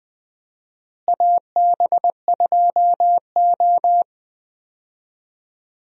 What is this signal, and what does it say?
Morse code, heard as a keyed tone, about 20 wpm, 700 Hz. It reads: AB2O
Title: USA Call Signs - 4 Characters - Repeated 3x - 20wpm